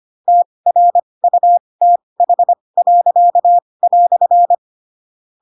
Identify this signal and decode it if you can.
Morse code, heard as a keyed tone, about 25 wpm, spaced as for 20 wpm, 700 Hz. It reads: TRUTH."
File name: Pinocchio / Pinocchio ch4 15 25 ik1hge